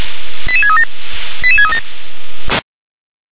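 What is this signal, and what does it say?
Five-tone selective call (ZVEI), heard as quick runs of five tones, tones of about 70 ms. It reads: 88637 88637
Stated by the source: ZVEI five-tone selective call received by SDR.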